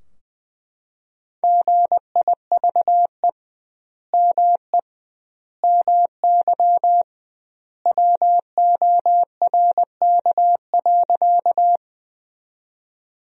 Morse code, heard as a keyed tone, about 20 wpm, 700 Hz. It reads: GIVE ME MY WORK.
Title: 2x Word-Spacing - Sentences from Top 200 Words - 20wpm